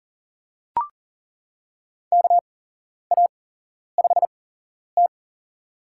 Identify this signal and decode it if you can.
Morse code, heard as a keyed tone, about 40 wpm, spaced as for 11 wpm, 700 Hz. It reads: KA5T
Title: USA Call Signs - 4 Characters - 8x Character Spacing - 40wpm